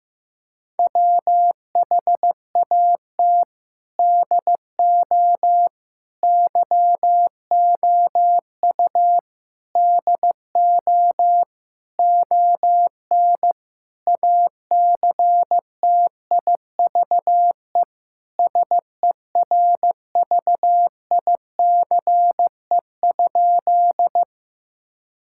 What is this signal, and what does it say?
Morse code, heard as a keyed tone, about 15 wpm, 700 Hz. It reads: WHAT DO YOU DO ON ACTIVE SERVICE?
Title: Sentences from Top 1000 Words - Review Part 6 - 15wpm